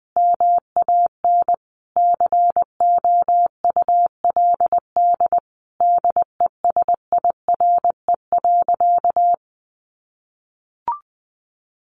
Morse code, heard as a keyed tone, about 19 wpm, 700 Hz. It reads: MAN COULD DESIRE.
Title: Sentences from Top 1100 Words 20wpm